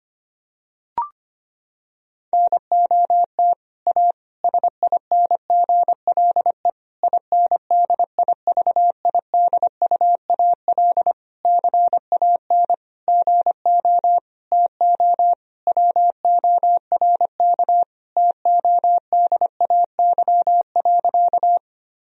Morse code, heard as a keyed tone, about 25 wpm, 700 Hz. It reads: NOT A SINGLE INDIVIDUAL CAN GO TO WORK TODAY.
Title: Sentences from Top 1000 Words - Review Part 4 - 25wpm